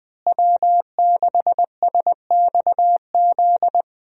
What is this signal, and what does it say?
Morse code, heard as a keyed tone, about 21 wpm, 700 Hz. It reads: W6SXZ
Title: USA Call Signs - 5 Characters - Repeated 3x - 20wpm